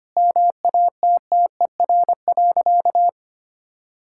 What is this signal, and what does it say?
Morse code, heard as a keyed tone, about 25 wpm, 700 Hz. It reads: MATTER.
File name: Sentences with 3-Word Phrases - 25wpm